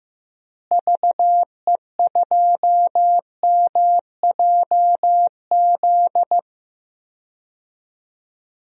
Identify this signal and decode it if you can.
Morse code, heard as a keyed tone, about 15 wpm, 700 Hz. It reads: VE2MJZ
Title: DXCC-100 Call Signs - 6 Characters - Repeated 2x - 15wpm